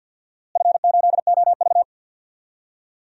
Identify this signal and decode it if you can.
Morse code, heard as a keyed tone, about 50 wpm, 700 Hz. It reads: F8OV
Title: DXCC-100 Call Signs - 4 Characters - Repeated 3x - 50wpm